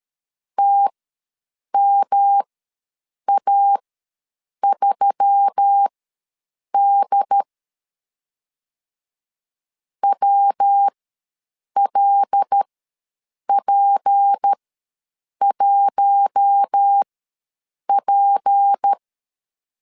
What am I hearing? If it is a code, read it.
Morse code, heard as a keyed tone, about 13 wpm, spaced as for 7 wpm, 800 Hz. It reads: TMA3D WLP1P
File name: LEZIONI / 6 PKX0 / 6 P ws 12wpm 3117